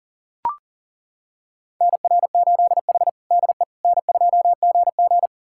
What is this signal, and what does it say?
Morse code, heard as a keyed tone, about 40 wpm, 700 Hz. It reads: NR8H DE N2GG
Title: USA Callsign DE Callsign - 4 Characters - 40wpm